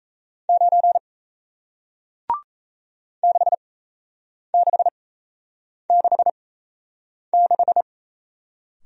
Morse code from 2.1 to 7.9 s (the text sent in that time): 6666